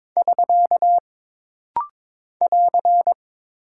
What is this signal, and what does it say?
Morse code, heard as a keyed tone, about 22 wpm, 700 Hz. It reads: <SK> <AR>
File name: practice / 22wpm / Single Punctuation-Prosign - Rapid-Fire - 22wpm